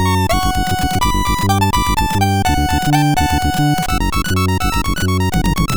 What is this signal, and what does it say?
There is no code.